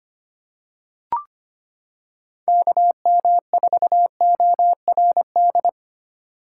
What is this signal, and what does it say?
Morse code, heard as a keyed tone, about 25 wpm, 700 Hz. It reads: KM4ORD
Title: K1USN SST QSO Practice - 25wpm